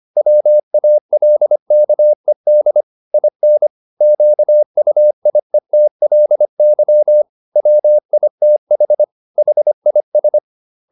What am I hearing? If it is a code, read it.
Morse code, heard as a keyed tone, about 25 wpm, 600 Hz. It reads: WALKED IN QUIETLY WITH HIS
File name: the old man and the sea part 1 25wpm